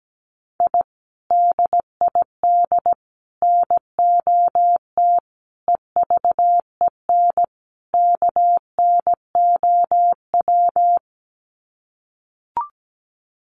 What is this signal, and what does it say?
Morse code, heard as a keyed tone, about 17 wpm, 700 Hz. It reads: I DID NOT EVEN KNOW
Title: Sets of 5 Words Encore 17wpm